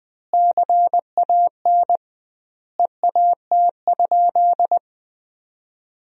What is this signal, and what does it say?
Morse code, heard as a keyed tone, about 20 wpm, 700 Hz. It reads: CAN EAT?
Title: 2x Word-Spacing - Sentences from Top 500 Words - 20wpm